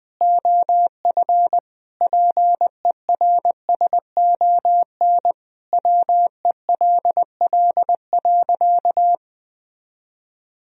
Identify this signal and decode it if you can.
Morse code, heard as a keyed tone, about 20 wpm, 700 Hz. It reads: OF PERSON WELL.
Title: Sentences from Top 1000 Words - Review Part 3 - 20wpm